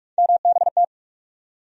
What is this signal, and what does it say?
Morse code, heard as a keyed tone, about 42 wpm, 700 Hz. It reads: MBT
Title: USA Call Sign Postfixes - 3 Letters 45wpm